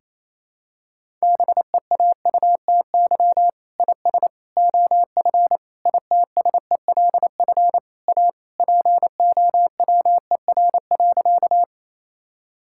Morse code, heard as a keyed tone, about 28 wpm, 700 Hz. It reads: BEAUTY IS OF ITSELF A POWER.
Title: Sentences from Top 1000 Words - Review Part 5 - 28wpm